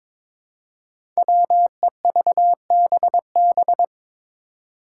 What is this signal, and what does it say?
Morse code, heard as a keyed tone, about 22 wpm, 700 Hz. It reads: WEVBB
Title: Five Challenging Letter-Number Combinations 22wpm